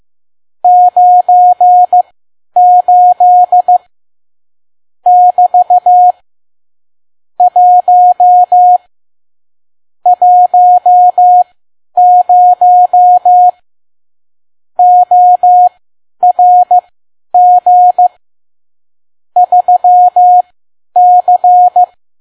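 Morse code, heard as a keyed tone, about 15 wpm, spaced as for 10 wpm, 720 Hz. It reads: 98 <BT> 1 10 ORG 3C
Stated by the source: Morse code practice at 10 WPM.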